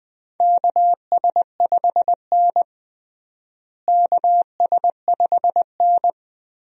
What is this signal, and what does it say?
Morse code, heard as a keyed tone, about 20 wpm, 700 Hz. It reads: KS5N KS5N
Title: USA Call Signs - 4 Characters - Repeated 3x - 20wpm